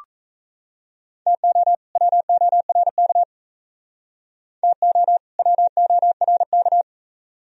Morse code, heard as a keyed tone, about 37 wpm, 700 Hz. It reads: TO WORK TO WORK